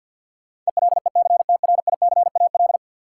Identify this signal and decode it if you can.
Morse code, heard as a keyed tone, about 53 wpm, 700 Hz. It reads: ELECTRICAL
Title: Sentences from Top 1000 Words - Review Part 6 - New Word Drill - 50wpm